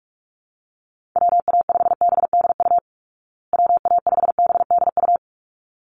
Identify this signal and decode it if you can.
Morse code, heard as a keyed tone, about 44 wpm, 700 Hz. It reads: WA5BDU WA5BDU
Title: Call Signs Repeated 3x - 45wpm